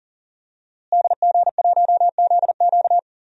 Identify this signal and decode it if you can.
Morse code, heard as a keyed tone, about 40 wpm, 700 Hz. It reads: DG1ZQ
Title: DXCC-100 Call Signs - 5 Characters - Repeated 2x - 40wpm